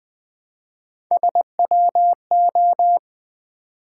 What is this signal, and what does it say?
Morse code, heard as a keyed tone, about 20 wpm, 700 Hz. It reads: SWO
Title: Three Letter-Number Combinations - 20wpm